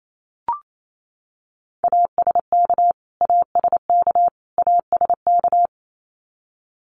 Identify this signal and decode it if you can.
Morse code, heard as a keyed tone, about 28 wpm, 700 Hz. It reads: ASK ASK ASK
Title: Top 300 Words Repeated 28wpm